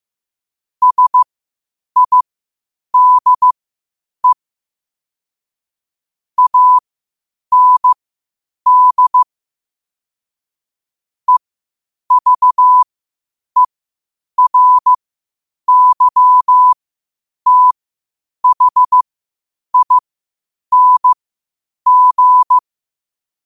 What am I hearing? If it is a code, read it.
Morse code, heard as a keyed tone, about 15 wpm, spaced as for 8 wpm, 1000 Hz. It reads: SIDE AND EVERYTHING
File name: qotdfiles / 8wpm